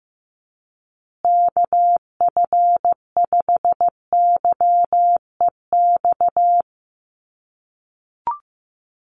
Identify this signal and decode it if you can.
Morse code, heard as a keyed tone, about 15 wpm, 700 Hz. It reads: KF5YEX
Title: USA Call Signs Repeated 3x 15wpm